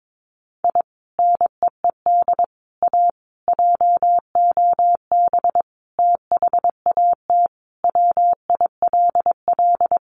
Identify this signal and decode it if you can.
Morse code, heard as a keyed tone, about 22 wpm, 700 Hz. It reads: I NEED A JOB THAT WILL
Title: Sentences from Top 1500 Words 22wpm